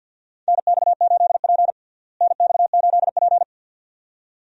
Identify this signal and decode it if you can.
Morse code, heard as a keyed tone, about 50 wpm, 700 Hz. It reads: NX8P NX8P
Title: USA Call Signs - 4 Characters - Repeated 2x - 50wpm